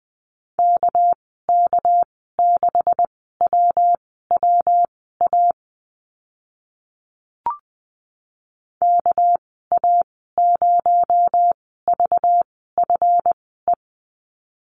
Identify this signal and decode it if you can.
Morse code, heard as a keyed tone, about 20 wpm, spaced as for 14 wpm, 700 Hz. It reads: KK6WWA KA0VFE
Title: 231 Lesson 37g - Letter x - Call Signs 20wpm